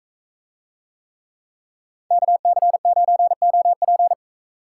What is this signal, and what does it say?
Morse code, heard as a keyed tone, about 43 wpm, 700 Hz. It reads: KC9OP